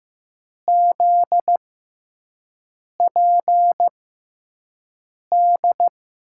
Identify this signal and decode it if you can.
Morse code, heard as a keyed tone, about 15 wpm, spaced as for 5 wpm, 700 Hz. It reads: ZPD
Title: USA Call Signs - 6x Character Spacing - 15wpm